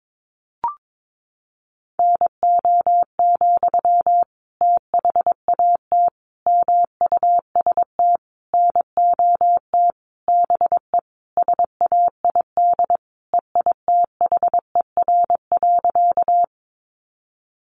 Morse code, heard as a keyed tone, about 22 wpm, 700 Hz. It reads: NO, THAT MUST NOT BE SAID EITHER.
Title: Sentences from Top 600 Words 22wpm